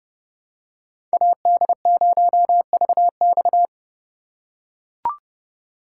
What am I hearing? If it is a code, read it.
Morse code, heard as a keyed tone, about 31 wpm, 700 Hz. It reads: AD0VX
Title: USA Call Signs - 5 Characters - 2x Character Spacing - 30wpm